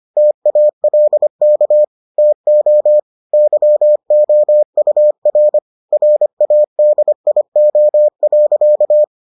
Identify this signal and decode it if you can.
Morse code, heard as a keyed tone, about 25 wpm, 600 Hz. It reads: TALK TO YOUR RADIO.